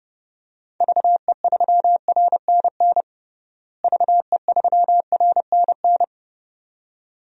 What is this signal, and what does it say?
Morse code, heard as a keyed tone, about 30 wpm, 700 Hz. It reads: VE3RNN VE3RNN
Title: DXCC-100 Call Signs - 6 Characters - Repeated 2x - 30wpm